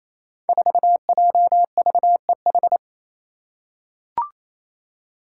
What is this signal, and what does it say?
Morse code, heard as a keyed tone, about 28 wpm, 700 Hz. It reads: 4JVEH
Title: Five Challenging Letter-Number Combinations 28wpm